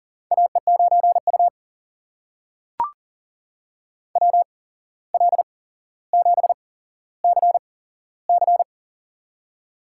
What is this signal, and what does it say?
Morse code, heard as a keyed tone, about 40 wpm, spaced as for 16 wpm, 700 Hz. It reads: AE9U WL7CC
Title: USA Call Signs - 8x Character Spacing - 40wpm